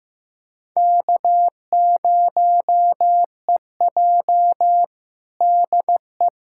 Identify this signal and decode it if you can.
Morse code, heard as a keyed tone, about 15 wpm, 700 Hz. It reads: K0EJ DE
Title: Practice QSO - Exchange 4 - Ending - 15wpm